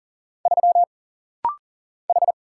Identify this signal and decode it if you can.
Morse code, heard as a keyed tone, about 40 wpm, 700 Hz. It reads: 3 H
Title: Single Letter-Number Alt - Rapid-Fire 40wpm